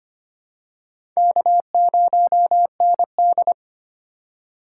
Morse code, heard as a keyed tone, about 25 wpm, 700 Hz. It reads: K0ND
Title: USA Call Signs - 4 Characters - 4x Character Spacing - 25wpm